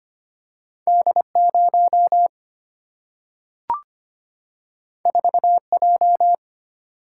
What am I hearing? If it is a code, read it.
Morse code, heard as a keyed tone, about 25 wpm, 700 Hz. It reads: D 0 4 J
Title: Single-Letter Single-Number 25wpm